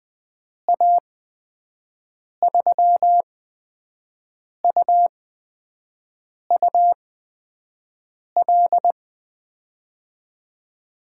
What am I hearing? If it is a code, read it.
Morse code, heard as a keyed tone, about 20 wpm, spaced as for 5 wpm, 700 Hz. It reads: A3UUL
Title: USA Call Signs - 8x Character Spacing - 20wpm